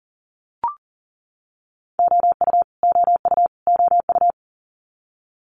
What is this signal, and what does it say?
Morse code, heard as a keyed tone, about 40 wpm, 700 Hz. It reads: OU OU OU
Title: Multi-letter Phonemes Repeated 3x 40wpm